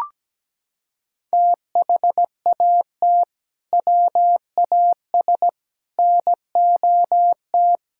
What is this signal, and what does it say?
Morse code, heard as a keyed tone, about 17 wpm, 700 Hz. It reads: THAT WAS NOT
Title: Sentences from Top 1000 Words - Review Part 5 - 17wpm